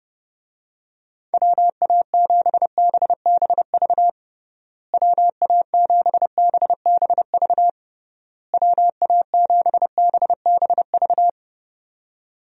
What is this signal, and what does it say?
Morse code, heard as a keyed tone, about 30 wpm, 700 Hz. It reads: WA7BBV WA7BBV WA7BBV
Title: USA Call Signs - 6 Characters - Repeated 3x - 30wpm